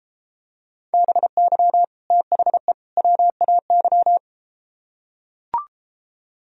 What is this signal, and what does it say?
Morse code, heard as a keyed tone, about 33 wpm, 700 Hz. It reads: BY THE WAY